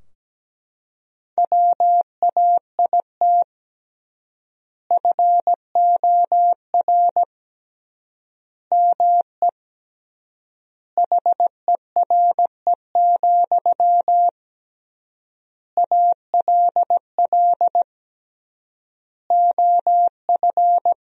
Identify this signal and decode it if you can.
Morse code, heard as a keyed tone, about 17 wpm, 700 Hz. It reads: WAIT FOR ME HERE, ALL OF
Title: 3x Word-Spacing - Sentences from Top 500 Words - 17wpm